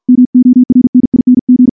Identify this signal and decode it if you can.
Morse code, heard as a keyed tone, about 51 wpm, 250 Hz. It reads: MORNING
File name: morse / morse-training-50WPM